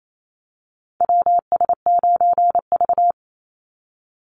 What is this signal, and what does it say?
Morse code, heard as a keyed tone, about 28 wpm, 700 Hz. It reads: WS9V
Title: Call Signs Repeated 3x - 28wpm